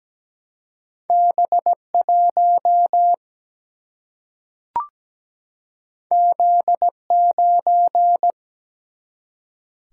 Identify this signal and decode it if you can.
Morse code, heard as a keyed tone, about 17 wpm, 700 Hz. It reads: B1 Z9
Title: Single-Letter Single-Number 17wpm